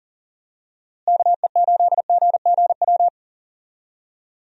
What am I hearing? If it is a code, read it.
Morse code, heard as a keyed tone, about 40 wpm, 700 Hz. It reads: KE8GGW
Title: USA Call Signs - 6 Characters - Repeated 3x - 40wpm